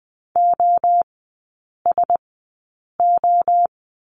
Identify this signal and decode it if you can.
Morse code, heard as a keyed tone, about 20 wpm, spaced as for 8 wpm, 700 Hz. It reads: OSO
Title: 121 Lesson 20e - Number 3 - Three Characters 20wpm